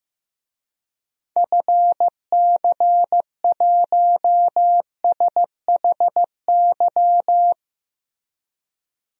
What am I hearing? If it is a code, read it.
Morse code, heard as a keyed tone, about 15 wpm, 700 Hz. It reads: FC1SHY
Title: DXCC-100 Call Signs - 6 Characters - 15wpm